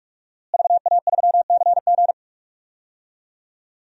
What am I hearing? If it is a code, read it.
Morse code, heard as a keyed tone, about 46 wpm, 700 Hz. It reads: VA3CG
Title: DXCC-100 Call Signs - 5 Characters - Repeated 2x - 45wpm